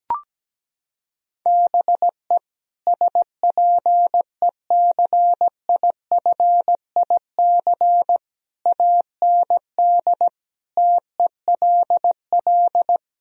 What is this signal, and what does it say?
Morse code, heard as a keyed tone, about 17 wpm, 700 Hz. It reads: BE SPECIFIC AND TELL
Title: Sentences from Top 1000 Words - Review Part 3 - 17wpm